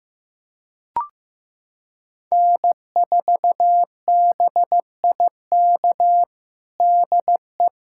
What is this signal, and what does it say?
Morse code, heard as a keyed tone, about 15 wpm, 700 Hz. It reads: N4BIK DE
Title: USA Callsign DE Callsign - 5 Characters - 15wpm